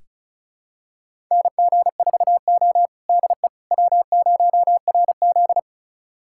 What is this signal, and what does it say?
Morse code, heard as a keyed tone, about 36 wpm, 700 Hz. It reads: NG4O DE W0RZ
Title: USA Callsign DE Callsign - 4 Characters - 35wpm